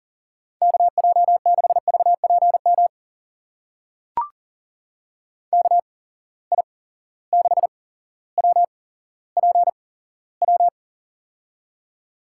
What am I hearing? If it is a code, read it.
Morse code, heard as a keyed tone, about 40 wpm, spaced as for 17 wpm, 700 Hz. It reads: KJ6VPM KI6WPW
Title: USA Call Signs - 6 Characters - 8x Character Spacing - 40wpm